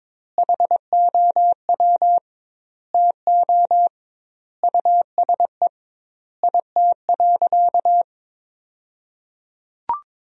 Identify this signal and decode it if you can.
Morse code, heard as a keyed tone, about 22 wpm, 700 Hz. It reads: HOW TO USE IT.
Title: 2x Word-Spacing - Sentences from Top 500 Words - 22wpm